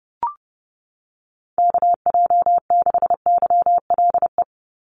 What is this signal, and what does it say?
Morse code, heard as a keyed tone, about 30 wpm, 700 Hz. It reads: KJ6YLE
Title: USA Call Signs 30wpm